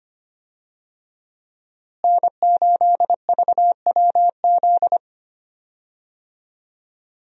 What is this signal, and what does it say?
Morse code, heard as a keyed tone, about 25 wpm, 700 Hz. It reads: N8VWZ
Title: USA Call Signs - 5 Characters - Repeated 3x - 25wpm